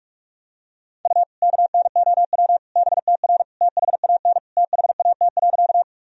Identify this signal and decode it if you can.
Morse code, heard as a keyed tone, about 41 wpm, 700 Hz. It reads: U KNOW BTR THAN THAT.
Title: Sentences from Top 400 Words and Abbreviations 45wpm